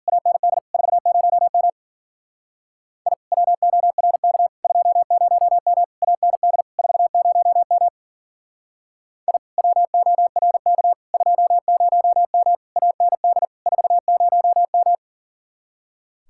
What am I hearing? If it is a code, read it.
Morse code, heard as a keyed tone, about 51 wpm, 700 Hz. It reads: AND 40M I WORK 20M AND 40M I WORK 20M AND 40M